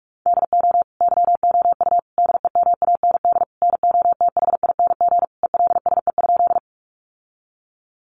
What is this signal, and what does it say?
Morse code, heard as a keyed tone, about 44 wpm, 700 Hz. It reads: DO YOU DEMAND NOTHING ELSE?
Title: Sentences from Top 800 Words 45wpm